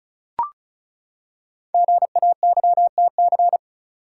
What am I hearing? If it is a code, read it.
Morse code, heard as a keyed tone, about 36 wpm, 700 Hz. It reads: GAYTC
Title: Code Groups - 5 Letters - 35wpm